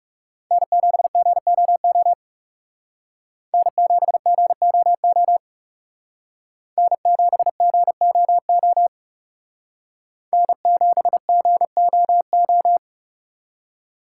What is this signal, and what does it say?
Morse code, heard as a keyed tone, about 38 wpm, 700 Hz. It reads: N7GOO N7GOO N7GOO N7GOO